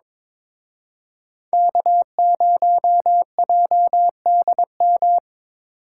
Morse code, heard as a keyed tone, about 25 wpm, 700 Hz. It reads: K0JDM
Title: Speed Racing - USA 1x3 Call Signs 22wpm